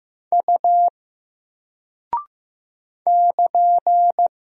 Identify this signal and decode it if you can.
Morse code, heard as a keyed tone, about 15 wpm, 700 Hz. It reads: U <KN>
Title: Single Character (Letter:Number:Punctuation:Prosign) Alt 15wpm